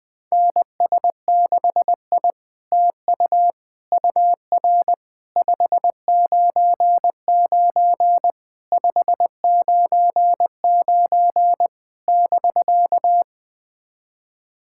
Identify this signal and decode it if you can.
Morse code, heard as a keyed tone, about 21 wpm, 700 Hz. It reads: NS6I TU UR 599 599 <BK>
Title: SOTA QSO Practice 20wpm